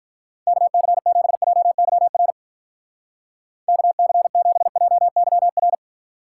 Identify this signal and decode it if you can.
Morse code, heard as a keyed tone, about 42 wpm, 700 Hz. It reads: KC7JYR KC7JYR
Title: Speed Racing - USA 2x3 Call Signs 35wpm